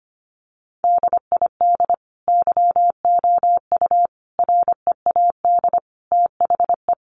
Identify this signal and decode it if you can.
Morse code, heard as a keyed tone, about 25 wpm, 700 Hz. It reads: DID YOU READ THE
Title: Sentences from Top 1200 Words 25wpm